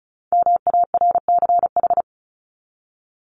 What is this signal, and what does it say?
Morse code, heard as a keyed tone, about 34 wpm, 700 Hz. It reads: MARCH
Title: Top 1501-1750 Words 35wpm